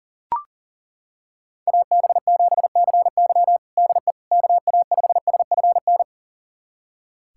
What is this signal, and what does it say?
Morse code, heard as a keyed tone, about 37 wpm, 700 Hz. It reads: AB7CY DE KA5SFN
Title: USA Callsign DE Callsign 40wpm